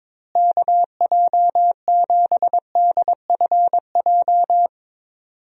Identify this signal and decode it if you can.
Morse code, heard as a keyed tone, about 22 wpm, 700 Hz. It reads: KJ7DFJ
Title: USA Call Signs - 6 Characters - Repeated 2x - 22wpm